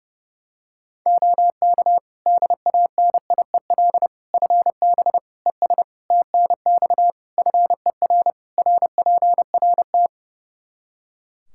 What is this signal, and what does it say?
Morse code, heard as a keyed tone, about 30 wpm, 700 Hz. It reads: OK DANIEL FB ES TNX FER RPRT
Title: K1USN SST QSO Practice 30wpm